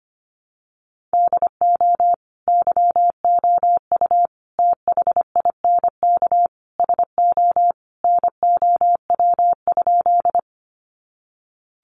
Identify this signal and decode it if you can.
Morse code, heard as a keyed tone, about 25 wpm, 700 Hz. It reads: DO YOU THINK SO NOW?
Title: Sentences from Top 200 Words 25wpm